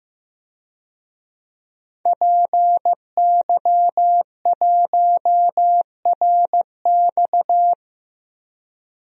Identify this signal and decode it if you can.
Morse code, heard as a keyed tone, about 15 wpm, 700 Hz. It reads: PY1RX
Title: DXCC-100 Call Signs - 5 Characters - 15wpm